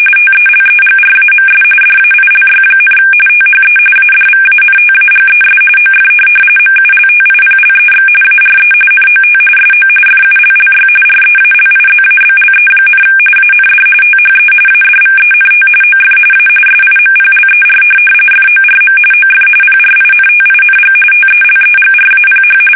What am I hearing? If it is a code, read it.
RTTY, about 45.45 baud. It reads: TELEGRAMS DAILY, THE ADVANTAGES OF MULTIPLEX ARE NOT SO FULLY REALIZABLE, AND THE ADVENT OF A SIMPLER FORM OF PRINTING TELEGRAPH